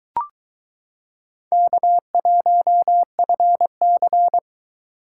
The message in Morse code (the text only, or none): K1FC